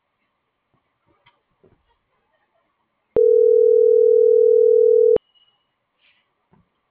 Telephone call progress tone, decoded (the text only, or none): ringback tone